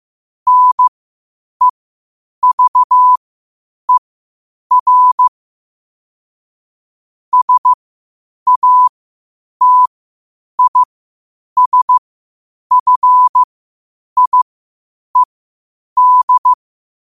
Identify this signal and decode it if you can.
Morse code, heard as a keyed tone, about 15 wpm, spaced as for 8 wpm, 1000 Hz. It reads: NEVER SATISFIED
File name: qotdfiles / 8wpm